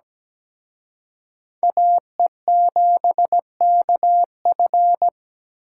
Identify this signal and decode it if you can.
Morse code, heard as a keyed tone, about 20 wpm, 700 Hz. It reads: AE7KF
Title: Speed Racing - USA 2x2 Call Signs 17wpm